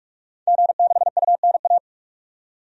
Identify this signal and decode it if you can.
Morse code, heard as a keyed tone, about 43 wpm, 700 Hz. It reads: G6UNA
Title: DXCC-100 Call Signs - 5 Characters - 45wpm